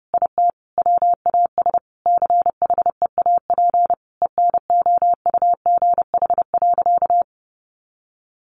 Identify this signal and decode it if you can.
Morse code, heard as a keyed tone, about 30 wpm, 700 Hz. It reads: IT WAS CHEAP ENOUGH.
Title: Sentences from Top 1200 Words 30wpm